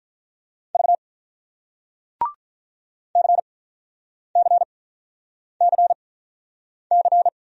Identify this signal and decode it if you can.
Morse code, heard as a keyed tone, about 48 wpm, spaced as for 9 wpm, 700 Hz. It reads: V CCCC